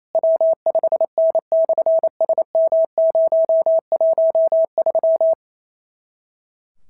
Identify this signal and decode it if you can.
Morse code, heard as a keyed tone, about 28 wpm, 650 Hz. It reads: W5N/SM013
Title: SOTA QSO Practice 28wpm